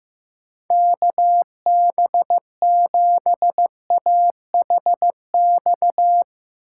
Morse code, heard as a keyed tone, about 15 wpm, 700 Hz. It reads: KB7AHX
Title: USA Call Signs - 6 Characters - 6x Character Spacing - 15wpm